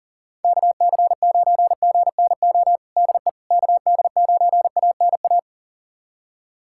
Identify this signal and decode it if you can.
Morse code, heard as a keyed tone, about 40 wpm, 700 Hz. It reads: KC9GNO DE KD9ANA
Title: USA Callsign DE Callsign - 6 Characters - 40wpm